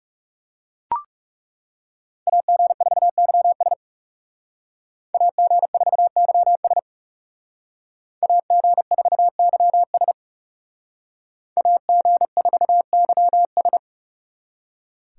Morse code, heard as a keyed tone, about 37 wpm, 700 Hz. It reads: AG4YS AG4YS AG4YS AG4YS
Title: Speed Racing - USA 2x2 Call Signs 30wpm